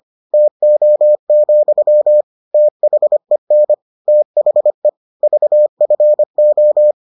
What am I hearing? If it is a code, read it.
Morse code, heard as a keyed tone, about 25 wpm, 600 Hz. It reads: TO, THEN THE VFO